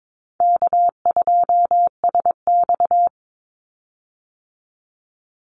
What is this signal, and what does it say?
Morse code, heard as a keyed tone, about 22 wpm, 700 Hz. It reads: K2SX
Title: Call Signs Repeated 3x - 22wpm